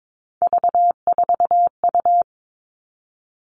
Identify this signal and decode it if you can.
Morse code, heard as a keyed tone, about 22 wpm, 700 Hz. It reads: V4U
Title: All Cut Numbers - 3-Digit 22wpm